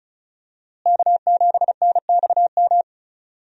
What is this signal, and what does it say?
Morse code, heard as a keyed tone, about 34 wpm, 700 Hz. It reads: K7NXM
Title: USA Call Signs - 5 Characters - Repeated 2x - 35wpm